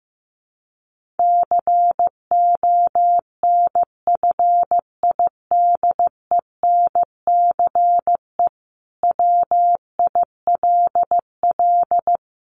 Morse code, heard as a keyed tone, about 15 wpm, 700 Hz. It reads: CONFIDENCE WILL
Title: Sentences from Top 1500 Words 15wpm